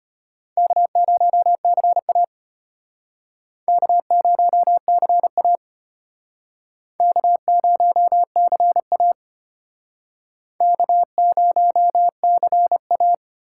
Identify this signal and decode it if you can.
Morse code, heard as a keyed tone, about 33 wpm, 700 Hz. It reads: K0CA K0CA K0CA K0CA